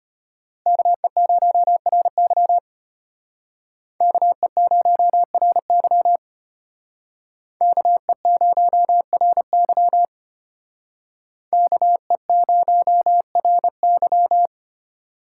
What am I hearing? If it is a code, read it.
Morse code, heard as a keyed tone, about 32 wpm, 700 Hz. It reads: KE0RY KE0RY KE0RY KE0RY